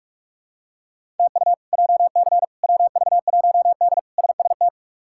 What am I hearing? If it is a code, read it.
Morse code, heard as a keyed tone, about 46 wpm, 700 Hz. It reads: TU JC WV1D SST
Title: K1USN SST QSO Practice - 45wpm